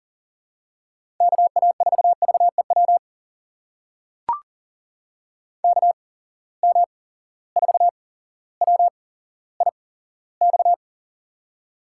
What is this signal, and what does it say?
Morse code, heard as a keyed tone, about 37 wpm, spaced as for 17 wpm, 700 Hz. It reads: KA4VEW KM4WIX
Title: USA Call Signs - 6 Characters - 8x Character Spacing - 40wpm